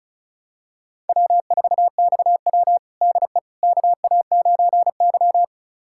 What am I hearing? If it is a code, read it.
Morse code, heard as a keyed tone, about 34 wpm, 700 Hz. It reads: W4XW DE KA9Y
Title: USA Callsign DE Callsign - 4 Characters - 35wpm